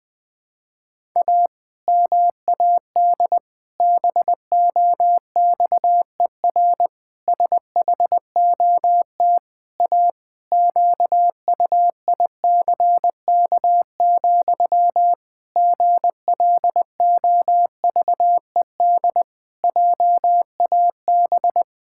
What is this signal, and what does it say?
Morse code, heard as a keyed tone, about 20 wpm, 700 Hz. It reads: A MAD BOXER SHOT A QUICK, GLOVED JAB
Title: Pangram Challenge v2 - 20wpm